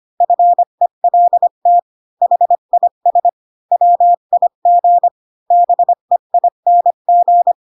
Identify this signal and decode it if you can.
Morse code, heard as a keyed tone, about 25 wpm, 700 Hz. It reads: FELT HIS WIG BEING
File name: Pinocchio / Pinocchio ch3 20 25 ik1hge